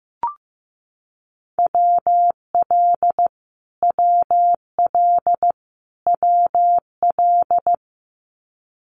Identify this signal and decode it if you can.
Morse code, heard as a keyed tone, about 15 wpm, 700 Hz. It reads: WL WL WL
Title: Abbreviations Repeated 3x - 15wpm